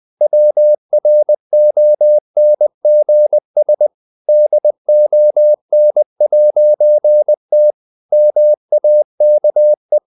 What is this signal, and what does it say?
Morse code, heard as a keyed tone, about 20 wpm, 600 Hz. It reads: WRONGS DON'T MAKE
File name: CW / Practice / EnglishSayings20